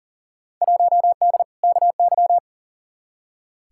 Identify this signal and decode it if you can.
Morse code, heard as a keyed tone, about 40 wpm, 700 Hz. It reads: 1D KY
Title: Field Day Exchanges - 40wpm